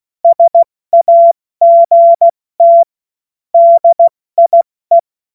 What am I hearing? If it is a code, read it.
Morse code, heard as a keyed tone, about 16 wpm, 680 Hz. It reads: SAGT DIE
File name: Morsekurs / Witze / Witz 06. Juli 2021, 70 F80